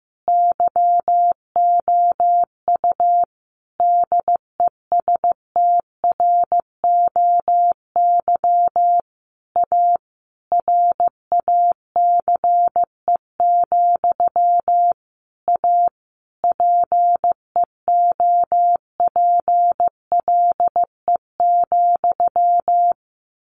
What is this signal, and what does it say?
Morse code, heard as a keyed tone, about 15 wpm, 700 Hz. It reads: YOU DESTROY A RACE, A PEOPLE,
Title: A Sound Of Thunder 15wpm